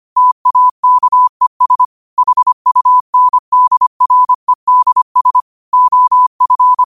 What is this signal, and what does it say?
Morse code, heard as a keyed tone, about 25 wpm, 1000 Hz. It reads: TAKES HUNDREDS OF
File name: qotdfiles / 25wpm